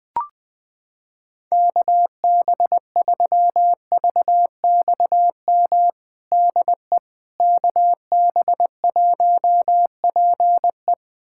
Morse code, heard as a keyed tone, about 21 wpm, 700 Hz. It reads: KB3VXM DE KB1PE
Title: USA Callsign DE Callsign 20wpm